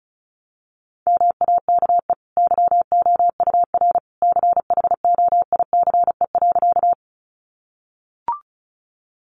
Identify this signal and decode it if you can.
Morse code, heard as a keyed tone, about 34 wpm, 700 Hz. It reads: MAKE YOUR CHOICE.
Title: Sentences from Top 900 Words 35wpm